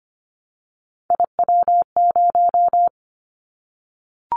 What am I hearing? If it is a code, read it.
Morse code, heard as a keyed tone, about 25 wpm, 700 Hz. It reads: IW0
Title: Callsign Prefixes for Top 50 DXCC - 25wpm